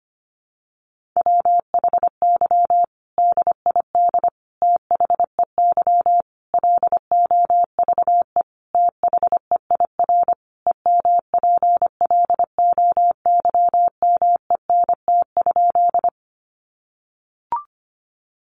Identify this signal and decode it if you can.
Morse code, heard as a keyed tone, about 25 wpm, 700 Hz. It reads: WHY DID THEY LOVE THEIR EMPLOYMENT?
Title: Sentences from Top 1500 Words 25wpm